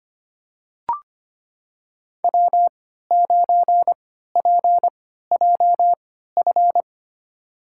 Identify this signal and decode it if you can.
Morse code, heard as a keyed tone, about 25 wpm, spaced as for 14 wpm, 700 Hz. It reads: W9PJF
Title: USA Call Signs - 5 Characters - 3x Character Spacing - 25wpm